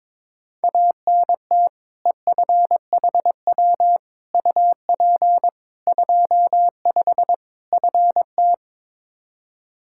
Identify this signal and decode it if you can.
Morse code, heard as a keyed tone, about 21 wpm, 700 Hz. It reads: ANT EFHW UP 25 FT
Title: Practice QSO - Exchange 2 - Info - 22wpm